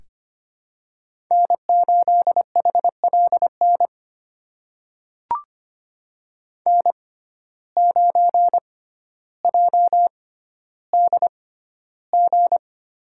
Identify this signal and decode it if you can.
Morse code, heard as a keyed tone, about 25 wpm, spaced as for 13 wpm, 700 Hz. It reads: N8HLN N9JDG
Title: USA Call Signs - 5 Characters - 6x Character Spacing - 25wpm